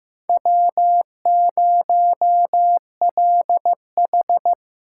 Morse code, heard as a keyed tone, about 15 wpm, 700 Hz. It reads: W0LH
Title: USA Callsign DE Callsign 15wpm